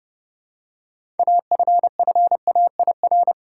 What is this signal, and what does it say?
Morse code, heard as a keyed tone, about 30 wpm, 700 Hz. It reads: AFFAIR
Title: Six Letter Words - 30wpm